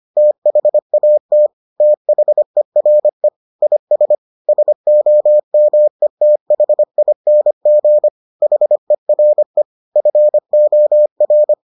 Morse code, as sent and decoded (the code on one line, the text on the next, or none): - .... .- - / - .... .-. / .. ... / ... --- -- . - .... .. -. --. / .... .-. / ..-. --- .-.
THAT THERE IS SOMETHING HERE FOR